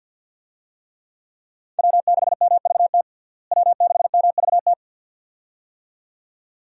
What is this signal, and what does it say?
Morse code, heard as a keyed tone, about 48 wpm, 700 Hz. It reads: W6MVT W6MVT
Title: USA Call Signs - 5 Characters - Repeated 2x - 50wpm